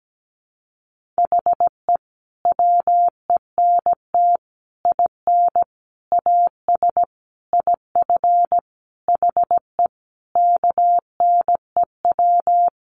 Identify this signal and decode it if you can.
Morse code, heard as a keyed tone, about 17 wpm, 700 Hz. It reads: HE WENT IN AS IF HE KNEW
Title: Sentences from Top 1400 Words 17wpm